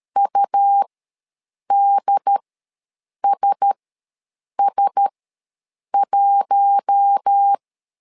Morse code, heard as a keyed tone, about 13 wpm, spaced as for 7 wpm, 800 Hz. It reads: UDSS1